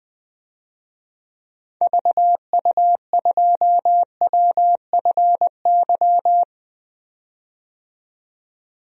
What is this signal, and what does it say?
Morse code, heard as a keyed tone, about 20 wpm, 700 Hz. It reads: VU2WFY
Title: DXCC-100 Call Signs - 6 Characters - 20wpm